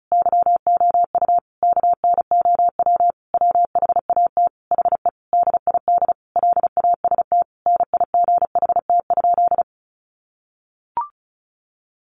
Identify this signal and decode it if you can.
Morse code, heard as a keyed tone, about 34 wpm, 700 Hz. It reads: YOU KNOW WHAT HE DID LAST NIGHT?
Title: Sentences from Top 300 Words 35wpm